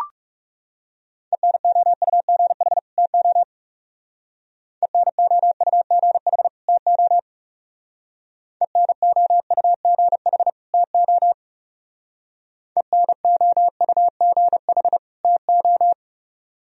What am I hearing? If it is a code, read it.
Morse code, heard as a keyed tone, about 37 wpm, 700 Hz. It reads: ENOUGH TO ENOUGH TO ENOUGH TO ENOUGH TO
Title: Speed-Racing - Sets of 2 Words 30wpm